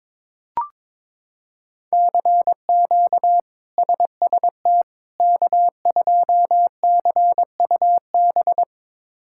Morse code, heard as a keyed tone, about 22 wpm, 700 Hz. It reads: CQ SST K2CUB
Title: K1USN SST QSO Practice - 22wpm